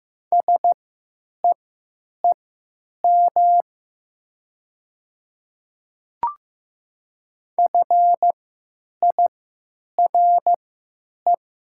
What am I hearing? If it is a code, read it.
Morse code, heard as a keyed tone, about 15 wpm, spaced as for 9 wpm, 700 Hz. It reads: SEEM FIRE
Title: 3x Character Spacing - Top 500 Words - 15wpm